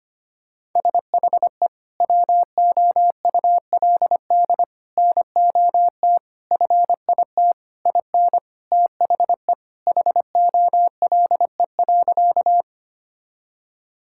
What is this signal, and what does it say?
Morse code, heard as a keyed tone, about 25 wpm, 700 Hz. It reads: SHE WOULD NOT FIT IN THE HOLE.
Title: Sentences from Top 1600 Words 25wpm